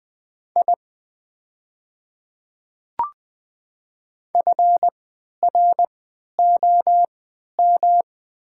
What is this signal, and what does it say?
Morse code, heard as a keyed tone, about 20 wpm, spaced as for 11 wpm, 700 Hz. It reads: I FROM
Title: 3x Character Spacing - Top 300 Words - 20wpm